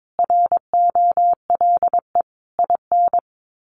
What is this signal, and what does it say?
Morse code, heard as a keyed tone, about 22 wpm, 700 Hz. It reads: ROLE IN
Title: Sets of 5 Words Encore2 22wpm